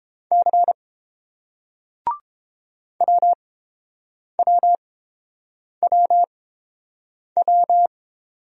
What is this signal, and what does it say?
Morse code, heard as a keyed tone, about 29 wpm, spaced as for 7 wpm, 700 Hz. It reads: C WWWW